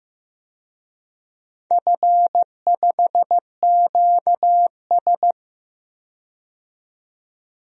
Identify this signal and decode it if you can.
Morse code, heard as a keyed tone, about 15 wpm, 700 Hz. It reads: F5QS
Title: DXCC-100 Call Signs - 4 Characters - 15wpm